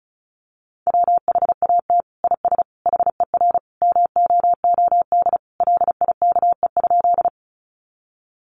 Morse code, heard as a keyed tone, about 34 wpm, 700 Hz. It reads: WHAT IS HER MOOD LIKE?
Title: Sentences from Top 1100 Words 35wpm